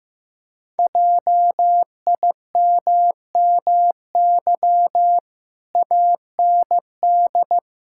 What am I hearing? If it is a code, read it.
Morse code, heard as a keyed tone, about 15 wpm, 700 Hz. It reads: JIMMY AND
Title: Pangram Challenge v2 - 15wpm